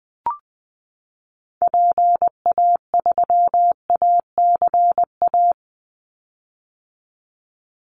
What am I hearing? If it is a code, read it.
Morse code, heard as a keyed tone, about 20 wpm, 700 Hz. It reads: PA3ACA
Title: Call Signs - 20wpm